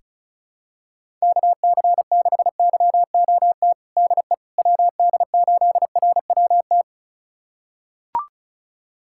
Morse code, heard as a keyed tone, about 36 wpm, 700 Hz. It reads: KC6YOT DE WD8RWT
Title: USA Callsign DE Callsign 35wpm